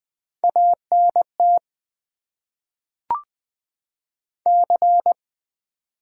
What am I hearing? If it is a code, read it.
Morse code, heard as a keyed tone, about 20 wpm, 700 Hz. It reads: ANT C
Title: Top 90 QSO Elements - 6x Character Spacing - 20wpm